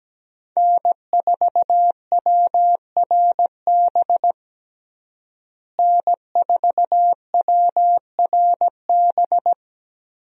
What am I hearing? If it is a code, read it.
Morse code, heard as a keyed tone, about 17 wpm, 700 Hz. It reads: N4WRB N4WRB
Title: USA Call Signs - 5 Characters - Repeated 3x - 17wpm